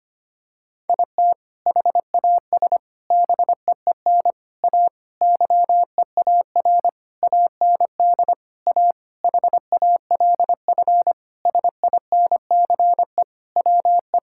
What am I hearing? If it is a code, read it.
Morse code, heard as a keyed tone, about 25 wpm, 700 Hz. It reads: IT HAS BEEN A YEAR AND A HALF SINCE WE
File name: practice / 25wpm / Sentences with 4-Word Phrases - 25wpm